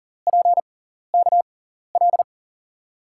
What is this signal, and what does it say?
Morse code, heard as a keyed tone, about 40 wpm, spaced as for 14 wpm, 700 Hz. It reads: PKL
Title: USA Call Signs - 5 Characters - 6x Character Spacing - 40wpm